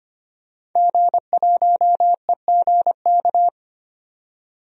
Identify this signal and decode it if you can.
Morse code, heard as a keyed tone, about 25 wpm, 700 Hz. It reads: G1EGK
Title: DXCC-100 Call Signs - 5 Characters - 25wpm